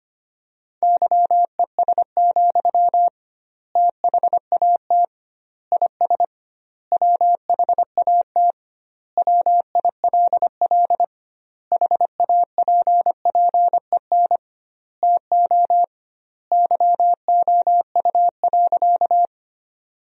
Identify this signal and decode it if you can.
Morse code, heard as a keyed tone, about 25 wpm, 700 Hz. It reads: YES, THAT IS WHAT WILL HAPPEN TO YOU.
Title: 2x Word-Spacing - Sentences from Top 500 Words - 25wpm